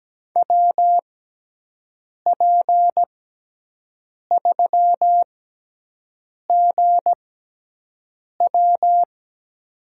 Morse code, heard as a keyed tone, about 17 wpm, spaced as for 6 wpm, 700 Hz. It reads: WP3GW
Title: USA Call Signs - 5 Characters - 6x Character Spacing - 17wpm